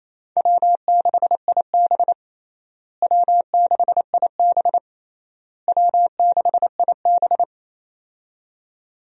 Morse code, heard as a keyed tone, about 28 wpm, 700 Hz. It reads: W6IB W6IB W6IB
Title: USA Call Signs - 4 Characters - Repeated 3x - 28wpm